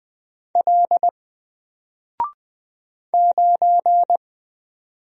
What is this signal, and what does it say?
Morse code, heard as a keyed tone, about 20 wpm, 700 Hz. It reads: L 9
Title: Single Letter-Number Alt 20wpm